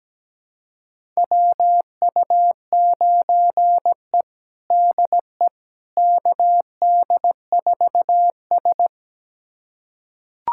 Morse code, heard as a keyed tone, about 17 wpm, 700 Hz. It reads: WU9E DE KD4S
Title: USA Callsign DE Callsign - 4 Characters - 17wpm